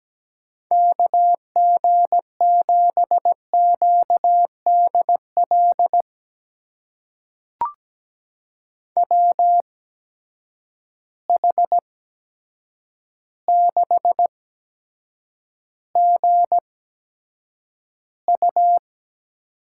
Morse code, heard as a keyed tone, about 17 wpm, spaced as for 8 wpm, 700 Hz. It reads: KG7QDL WH6GU
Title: USA Call Signs - 8x Character Spacing - 17wpm